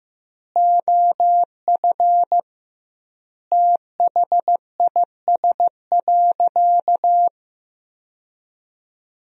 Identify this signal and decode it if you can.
Morse code, heard as a keyed tone, about 15 wpm, 700 Hz. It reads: OF THIS.
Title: 2x Word-Spacing - Sentences from Top 400 Words - 15wpm